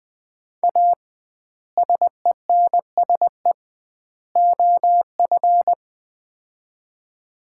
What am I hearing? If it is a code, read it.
Morse code, heard as a keyed tone, about 20 wpm, 700 Hz. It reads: A SENSE OF
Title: 2x Word-Spacing - Sets of 3 Words - 20wpm